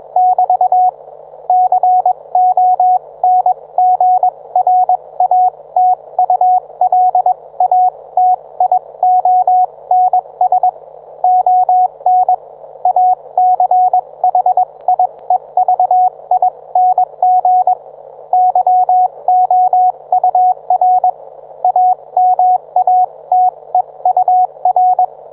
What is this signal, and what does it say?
Morse code, heard as a keyed tone, about 21 wpm, spaced as for 18 wpm, 730 Hz, with radio noise. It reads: <BT> CONGRATULATIONS ON ACHIEVING YOUR AMATEUR
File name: book